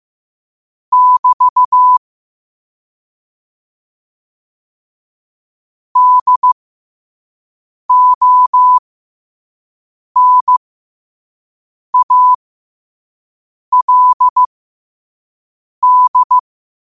Morse code, heard as a keyed tone, about 15 wpm, spaced as for 5 wpm, 1000 Hz. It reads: <BT> DONALD
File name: qotdfiles / 5wpm